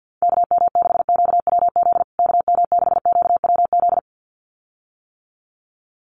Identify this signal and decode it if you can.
Morse code, heard as a keyed tone, about 50 wpm, 700 Hz. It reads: KM6QWZ KM6QWZ
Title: USA Call Signs Repeated 2x 50wpm